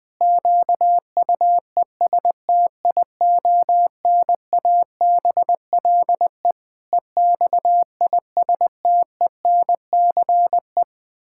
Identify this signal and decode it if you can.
Morse code, heard as a keyed tone, about 20 wpm, 700 Hz. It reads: QUESTIONABLE EXISTENCE
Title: Pangram Challenge v2 - 20wpm